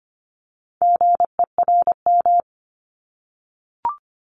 Morse code, heard as a keyed tone, about 25 wpm, 700 Hz. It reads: GERM
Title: Multi-letter Phonemes Repeated 3x 25wpm